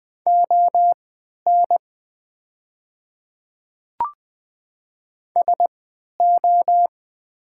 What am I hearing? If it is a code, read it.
Morse code, heard as a keyed tone, about 20 wpm, spaced as for 11 wpm, 690 Hz. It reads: ON SO
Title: 3x Character Spacing - Top 100 Words - 20wpm